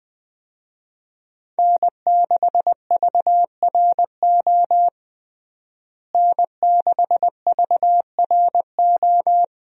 Morse code, heard as a keyed tone, about 21 wpm, 700 Hz. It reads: N6VRO N6VRO
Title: USA Call Signs - 5 Characters - Repeated 3x - 20wpm